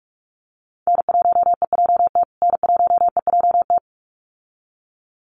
Morse code, heard as a keyed tone, about 44 wpm, 700 Hz. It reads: N1EJT N1EJT
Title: USA Call Signs Repeated 2x 45wpm